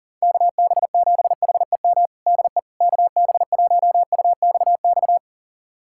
Morse code, heard as a keyed tone, about 40 wpm, 700 Hz. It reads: KB7HEM DE KB1UXX
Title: USA Callsign DE Callsign - 6 Characters - 40wpm